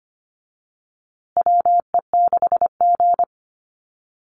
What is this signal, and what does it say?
Morse code, heard as a keyed tone, about 25 wpm, 700 Hz. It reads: WE6G
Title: USA Call Signs Repeated 3x 25wpm